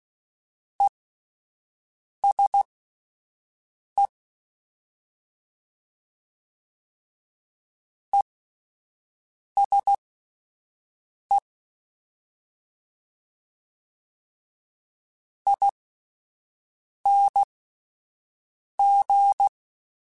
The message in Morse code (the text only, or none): ESE ESE ING